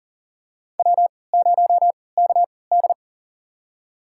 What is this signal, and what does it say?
Morse code, heard as a keyed tone, about 42 wpm, spaced as for 23 wpm, 700 Hz. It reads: W0KD
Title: USA Call Signs - 4 Characters - 3x Character Spacing - 40wpm